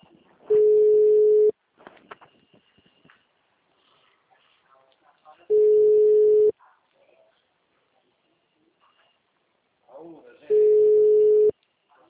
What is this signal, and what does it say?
Ringback tone (Europe, 425 Hz), with speech.
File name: out- -2011- - - .23591